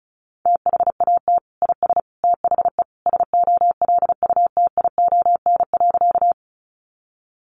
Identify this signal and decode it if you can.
Morse code, heard as a keyed tone, about 34 wpm, 700 Hz. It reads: THAT IS THE SOLUTION.
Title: Sentences from Top 1300 Words 35wpm